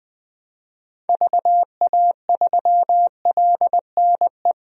Morse code, heard as a keyed tone, about 21 wpm, 700 Hz. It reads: VA3LNE
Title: DXCC-100 Call Signs - 6 Characters - 20wpm